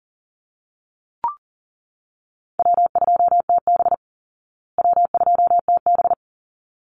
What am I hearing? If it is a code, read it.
Morse code, heard as a keyed tone, about 40 wpm, 700 Hz. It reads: W2TB W2TB
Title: Easy Call Signs Repeated 2x - 40wpm